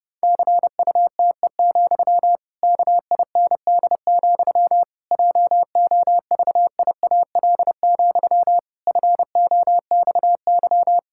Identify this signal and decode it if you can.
Morse code, heard as a keyed tone, about 30 wpm, 700 Hz. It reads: CUTE, KIND, JOVIAL, FOXY
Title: Pangram Challenge v2 - 30wpm